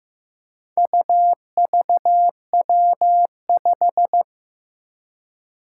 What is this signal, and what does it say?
Morse code, heard as a keyed tone, about 15 wpm, 700 Hz. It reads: UVW5
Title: Four Challenging Letter-Number Combinations 15wpm